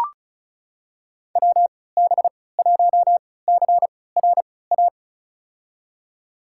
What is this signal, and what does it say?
Morse code, heard as a keyed tone, about 36 wpm, spaced as for 20 wpm, 700 Hz. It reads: WB1CRA
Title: USA Call Signs - 6 Characters - 3x Character Spacing - 35wpm